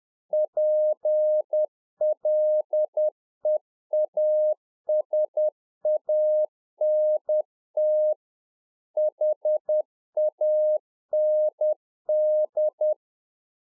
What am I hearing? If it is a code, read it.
Morse code, heard as a keyed tone, about 10 wpm, 610 Hz. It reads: PLEASANT HAND
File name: Czech Army Morse Keys 10WPM